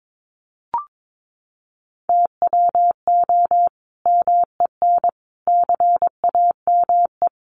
Morse code, heard as a keyed tone, about 22 wpm, 700 Hz. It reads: TWO MEN CAME
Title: Sentences from Top 600 Words 22wpm